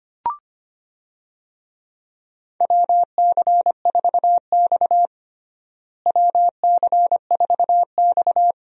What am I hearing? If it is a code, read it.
Morse code, heard as a keyed tone, about 25 wpm, 700 Hz. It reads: WC4X WC4X
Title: USA Call Signs - 4 Characters - Repeated 2x - 25wpm